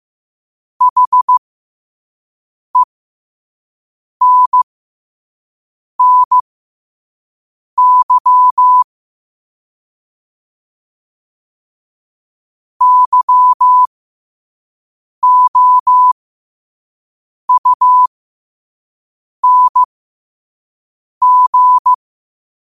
Morse code, heard as a keyed tone, about 15 wpm, spaced as for 5 wpm, 1000 Hz. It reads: HENNY YOUNG
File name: qotdfiles / 5wpm